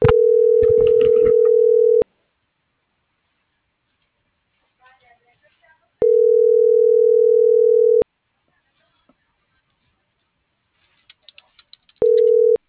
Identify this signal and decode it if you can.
Ringback tone (North America), 440 + 480 Hz.